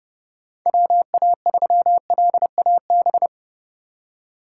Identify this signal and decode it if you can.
Morse code, heard as a keyed tone, about 30 wpm, 700 Hz. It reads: WA3LAB
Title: K1USN SST QSO Practice - 30wpm